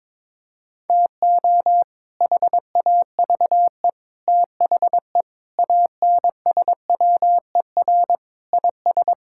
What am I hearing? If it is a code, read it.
Morse code, heard as a keyed tone, about 22 wpm, 700 Hz. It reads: TO HAVE THE ANSWER IS
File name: practice / 22wpm / Sentences with 3-Word Phrases - 22wpm